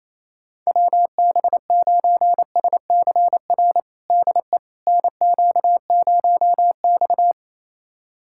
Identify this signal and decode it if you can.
Morse code, heard as a keyed tone, about 28 wpm, 700 Hz. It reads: WB9SCR DE NQ0X
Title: USA Callsign DE Callsign 28wpm